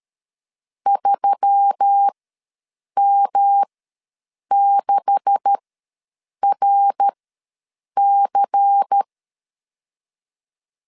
Morse code, heard as a keyed tone, about 13 wpm, spaced as for 7 wpm, 800 Hz. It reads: 3M6RC
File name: LEZIONI / 10 SIGNS / 10D COMA ws 12wpm 3618